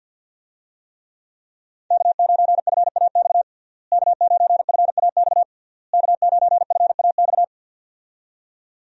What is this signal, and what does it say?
Morse code, heard as a keyed tone, about 51 wpm, 700 Hz. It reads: K9FAX K9FAX K9FAX
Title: USA Call Signs - 5 Characters - Repeated 3x - 50wpm